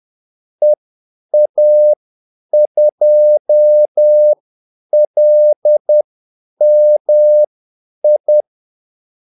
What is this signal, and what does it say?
Morse code, heard as a keyed tone, about 10 wpm, spaced as for 8 wpm, 600 Hz. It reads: EA2LMI
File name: Callsigns-10-8wpm